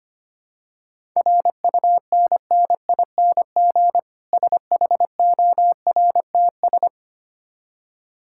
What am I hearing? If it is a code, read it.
Morse code, heard as a keyed tone, about 25 wpm, 700 Hz. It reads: RUNNING SHORTS
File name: practice / 25wpm / Difficult Words with Hints - 25wpm